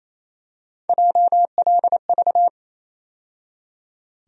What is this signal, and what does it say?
Morse code, heard as a keyed tone, about 28 wpm, 700 Hz. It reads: JLV
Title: USA Call Sign Postfixes - 3 Letters 28wpm